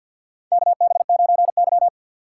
Morse code, heard as a keyed tone, about 50 wpm, 700 Hz. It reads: KB9Y
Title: USA Call Signs - 4 Characters - 3x Character Spacing - 50wpm